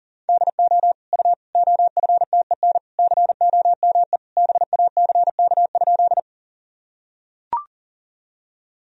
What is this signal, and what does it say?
Morse code, heard as a keyed tone, about 40 wpm, 700 Hz. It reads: DO U OFTEN COME BACK?
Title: Sentences from Top 300 Words and Abbreviations 40wpm